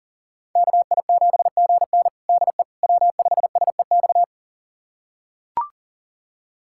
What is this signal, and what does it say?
Morse code, heard as a keyed tone, about 40 wpm, 700 Hz. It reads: KI7GN DE W5SEX
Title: USA Callsign DE Callsign - 5 Characters - 40wpm